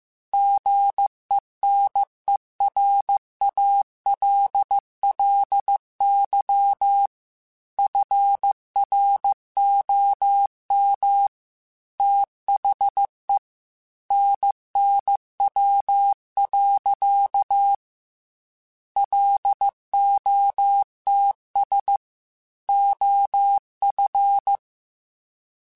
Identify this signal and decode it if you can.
Morse code, heard as a keyed tone, about 15 wpm, 800 Hz. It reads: GENERALLY FROM THE NNW. LOTS OF
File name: Weather Sets / Weather Sets-mp3s / Weather 5 12 @13